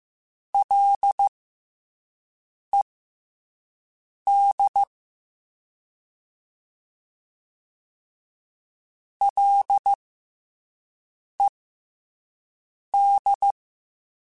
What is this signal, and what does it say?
Morse code, heard as a keyed tone, about 15 wpm, spaced as for 5 wpm, 800 Hz. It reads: LED LED
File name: F and L / F and L mp3 / F and L Prefixes 3x2 Twice @5